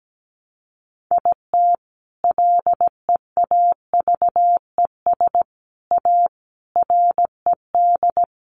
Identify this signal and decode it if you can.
Morse code, heard as a keyed tone, about 17 wpm, 700 Hz. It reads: IT LEAVES A RED
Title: A Sound Of Thunder 17wpm